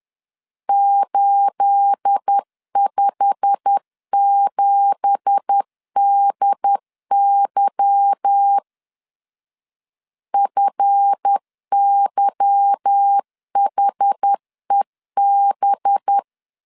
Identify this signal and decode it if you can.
Morse code, heard as a keyed tone, about 10 wpm, 800 Hz. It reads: 857DY FYHEB
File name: LEZIONI / 10 SIGNS / 10B INT 10wpm 2844